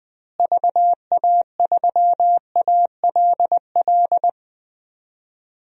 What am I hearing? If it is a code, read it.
Morse code, heard as a keyed tone, about 20 wpm, 700 Hz. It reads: VA3ALL
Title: DXCC-100 Call Signs - 6 Characters - Repeated 2x - 20wpm